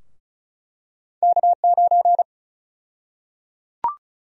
Morse code, heard as a keyed tone, about 36 wpm, 700 Hz. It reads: K9
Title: USA Call Sign Prefixes - 35wpm